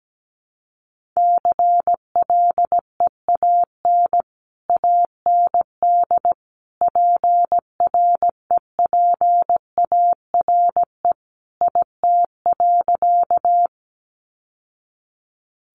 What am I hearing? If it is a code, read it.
Morse code, heard as a keyed tone, about 17 wpm, 700 Hz. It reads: CLEAN AND PREPARE IT.
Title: Sentences from Top 1100 Words 17wpm